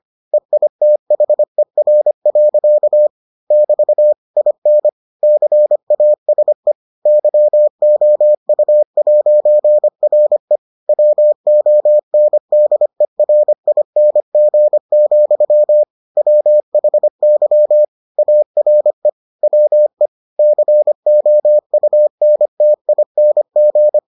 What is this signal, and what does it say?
Morse code, heard as a keyed tone, about 25 wpm, 600 Hz. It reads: EITHER. <BT> IN CASE YOU'RE WONDERING, WHY ARE WE COUNTING